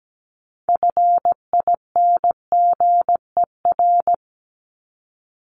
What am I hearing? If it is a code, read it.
Morse code, heard as a keyed tone, about 17 wpm, 700 Hz. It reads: FINGER
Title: Top 1501-1750 Words 17wpm